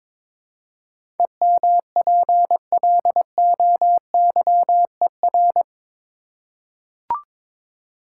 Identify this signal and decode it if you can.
Morse code, heard as a keyed tone, about 22 wpm, 700 Hz. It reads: EMPLOYER
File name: practice / 22wpm / Eight Letter Words - 22wpm